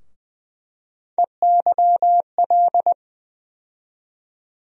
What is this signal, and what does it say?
Morse code, heard as a keyed tone, about 20 wpm, 700 Hz. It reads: EYL
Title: USA Call Sign Postfixes - 3 Letters 20wpm